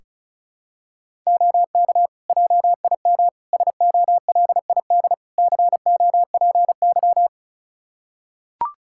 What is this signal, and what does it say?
Morse code, heard as a keyed tone, about 36 wpm, 700 Hz. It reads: OK JIM SOLID COPY
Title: Practice QSO - Exchange 3 - More Info - 35wpm